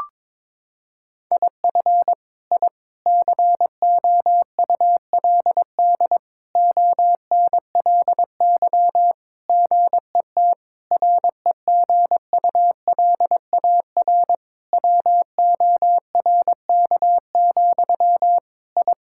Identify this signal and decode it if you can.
Morse code, heard as a keyed tone, about 22 wpm, 700 Hz. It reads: IF I COULD ONLY GET REGULAR WORK, I
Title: Sentences from Top 1000 Words - Review Part 6 - 22wpm